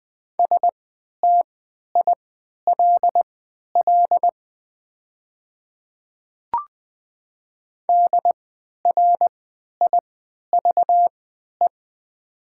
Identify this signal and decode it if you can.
Morse code, heard as a keyed tone, about 21 wpm, spaced as for 11 wpm, 700 Hz. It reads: STILL DRIVE
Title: 3x Character Spacing - Top 500 Words - 20wpm